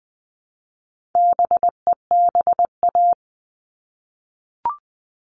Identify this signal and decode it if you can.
Morse code, heard as a keyed tone, about 20 wpm, 700 Hz. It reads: BEBA
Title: All Cut Numbers - 4-Digit 20wpm